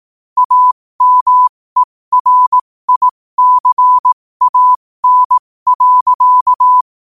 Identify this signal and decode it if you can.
Morse code, heard as a keyed tone, about 18 wpm, spaced as for 15 wpm, 1000 Hz. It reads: AMERICAN.
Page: New quotes every day in morse code at 15 Words per minute.